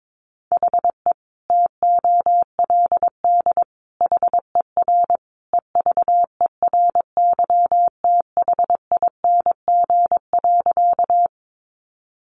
Morse code, heard as a keyed tone, about 22 wpm, 700 Hz. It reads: HE TOLD HER EVERYTHING.
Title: Sentences from Top 700 Words 22wpm